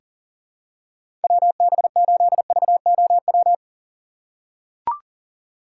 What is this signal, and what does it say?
Morse code, heard as a keyed tone, about 40 wpm, 700 Hz. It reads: WB8VOW
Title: USA Call Signs - 6 Characters - 2x Character Spacing - 40wpm